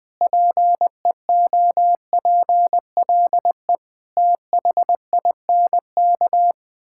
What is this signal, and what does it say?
Morse code, heard as a keyed tone, about 20 wpm, 700 Hz. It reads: PEOPLE THINK
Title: Sentences from Top 1000 Words - Review Part 5 - 20wpm